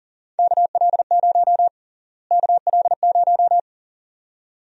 Morse code, heard as a keyed tone, about 40 wpm, 700 Hz. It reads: KL0 KL0
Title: USA Call Sign Prefixes - Repeated 2x - 40wpm